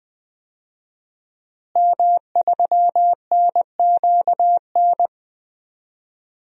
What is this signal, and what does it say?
Morse code, heard as a keyed tone, about 21 wpm, 700 Hz. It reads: M3NQN
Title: DXCC-100 Call Signs - 5 Characters - Repeated 2x - 20wpm